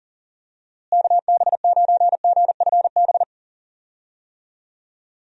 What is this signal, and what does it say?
Morse code, heard as a keyed tone, about 40 wpm, 700 Hz. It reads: KB9GFB
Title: USA Call Signs - 6 Characters - Repeated 2x - 40wpm